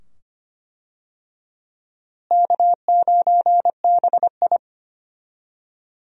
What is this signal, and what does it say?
Morse code, heard as a keyed tone, about 25 wpm, 700 Hz. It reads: K9BI
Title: USA Call Signs - 4 Characters - Repeated 2x - 25wpm